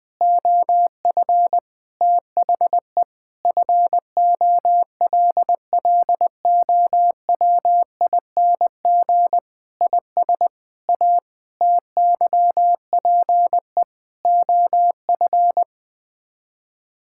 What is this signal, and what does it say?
Morse code, heard as a keyed tone, about 21 wpm, 700 Hz. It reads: OF THE FOLLOWING IS A TYPE OF
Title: Sets of 8 Words - 20wpm